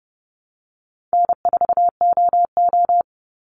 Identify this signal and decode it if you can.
Morse code, heard as a keyed tone, about 30 wpm, 700 Hz. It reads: N4OO
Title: Call Signs Repeated 3x - 30wpm